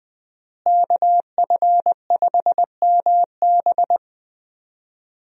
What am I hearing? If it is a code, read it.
Morse code, heard as a keyed tone, about 20 wpm, 700 Hz. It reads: KF5MB
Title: USA Call Signs - 5 Characters - Repeated 2x - 20wpm